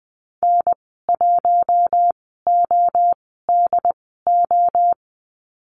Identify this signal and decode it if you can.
Morse code, heard as a keyed tone, about 20 wpm, spaced as for 14 wpm, 700 Hz. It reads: N1ODO
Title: 069 Lesson 12g - Letter d - Call Signs 20wpm